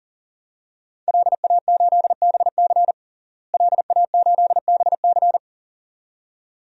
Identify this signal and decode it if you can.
Morse code, heard as a keyed tone, about 40 wpm, 700 Hz. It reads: LA8BC LA8BC
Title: DXCC-100 Call Signs - 5 Characters - Repeated 2x - 40wpm